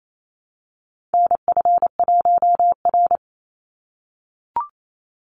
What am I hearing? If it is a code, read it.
Morse code, heard as a keyed tone, about 28 wpm, 700 Hz. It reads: NF1R
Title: NA SOTA Chaser Callsigns 28wpm